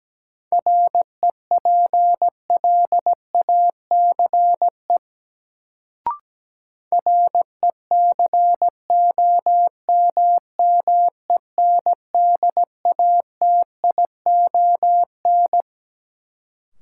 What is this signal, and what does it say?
Morse code, heard as a keyed tone, about 17 wpm, 700 Hz. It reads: REPLACE RECOMMENDATION
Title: Sentences from Top 1400 Words - New Word Drill Alt - 17wpm